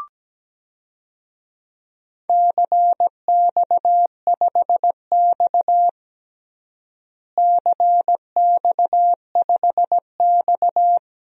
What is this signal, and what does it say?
Morse code, heard as a keyed tone, about 17 wpm, 700 Hz. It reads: CX5X CX5X
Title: DXCC-100 Call Signs - 4 Characters - Repeated 2x - 17wpm